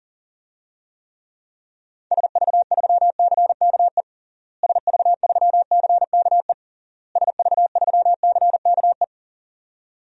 Morse code, heard as a keyed tone, about 37 wpm, 700 Hz. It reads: SV3CKE SV3CKE SV3CKE
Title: DXCC-100 Call Signs - 6 Characters - Repeated 3x - 40wpm